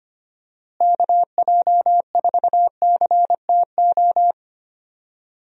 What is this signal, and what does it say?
Morse code, heard as a keyed tone, about 25 wpm, 700 Hz. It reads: KJ4CTO
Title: USA Call Signs - 6 Characters - Repeated 2x - 25wpm